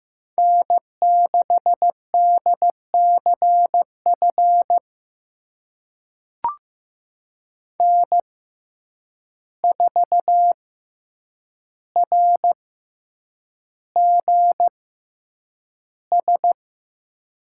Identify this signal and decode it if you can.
Morse code, heard as a keyed tone, about 15 wpm, spaced as for 8 wpm, 700 Hz. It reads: N6DCF N4RGS
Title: USA Call Signs - 5 Characters - 6x Character Spacing - 15wpm